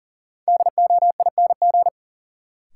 Morse code, heard as a keyed tone, about 40 wpm, 700 Hz. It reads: DOING
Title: Sentences from Top 1400 Words - New Word Drill Alt - 40wpm